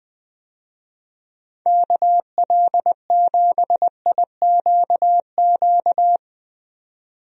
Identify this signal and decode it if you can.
Morse code, heard as a keyed tone, about 20 wpm, 700 Hz. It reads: KL7IQQ
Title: DXCC-100 Call Signs - 6 Characters - 20wpm